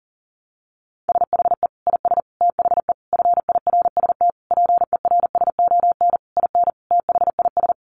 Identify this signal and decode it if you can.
Morse code, heard as a keyed tone, about 40 wpm, 700 Hz. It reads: SHE IS THE FIRST PERSON IN THIS
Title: Sentences from Top 400 Words 40wpm